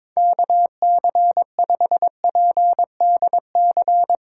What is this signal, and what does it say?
Morse code, heard as a keyed tone, about 22 wpm, 700 Hz. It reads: KC5PDC
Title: USA Call Signs - 6 Characters - 4x Character Spacing - 22wpm